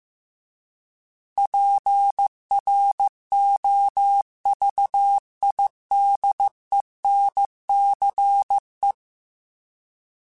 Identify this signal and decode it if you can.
Morse code, heard as a keyed tone, about 15 wpm, 800 Hz. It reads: PROVIDENCE
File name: fromSMs / FromSMs mp3s / Cities@13